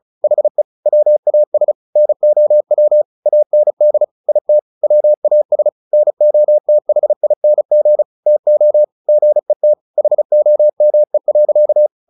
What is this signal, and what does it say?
Morse code, heard as a keyed tone, about 34 wpm, 600 Hz. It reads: HE WAS NOW AND IT WAS NOTHING TO GET HOME.
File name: the old man and the sea 2 35wpm